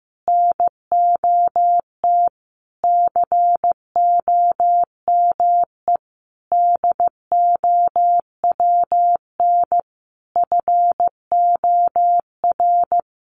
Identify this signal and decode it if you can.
Morse code, heard as a keyed tone, about 15 wpm, 700 Hz. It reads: NOT COME DOWN FOR
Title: Sentences from Top 1500 Words 15wpm